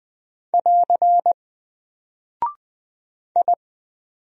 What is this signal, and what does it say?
Morse code, heard as a keyed tone, about 20 wpm, 700 Hz. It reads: <AR> I
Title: Single Character (Letter:Number:Punctuation:Prosign) Alt 20wpm